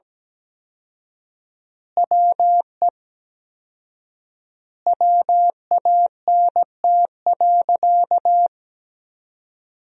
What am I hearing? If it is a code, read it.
Morse code, heard as a keyed tone, about 17 wpm, 700 Hz. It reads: WE WANT.
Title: 4x Word-Spacing - Sentences from Top 200 Words - 17wpm